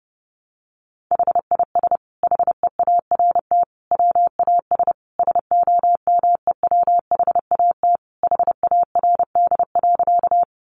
Morse code, heard as a keyed tone, about 30 wpm, 700 Hz. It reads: HIS HEART WAS SOMEWHAT HARD.
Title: Sentences from Top 1300 Words 30wpm